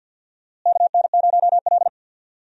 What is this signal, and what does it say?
Morse code, heard as a keyed tone, about 52 wpm, 700 Hz. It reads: KN0L
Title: USA Call Signs - 4 Characters - Repeated 3x - 50wpm